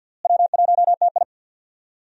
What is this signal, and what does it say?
Morse code, heard as a keyed tone, about 48 wpm, 700 Hz. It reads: W1TI
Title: USA Call Signs - 4 Characters - Repeated 2x - 50wpm